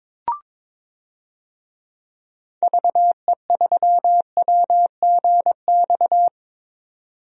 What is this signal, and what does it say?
Morse code, heard as a keyed tone, about 22 wpm, 700 Hz. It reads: VE3WGX
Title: DXCC-100 Call Signs - 6 Characters - 22wpm